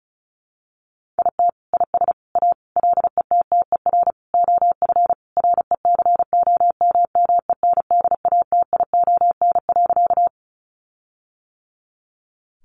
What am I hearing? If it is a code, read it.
Morse code, heard as a keyed tone, about 34 wpm, 700 Hz. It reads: IT IS A LETTER OF RECOMMENDATION.
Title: Sentences from Top 1400 Words 35wpm